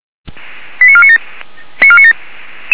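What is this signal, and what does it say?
Five-tone selective call (ZVEI), heard as quick runs of five tones, tones of about 70 ms. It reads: 89497
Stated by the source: ZVEI five-tone selective call received via SDR, fire brigade feuerwehr-orlamuende.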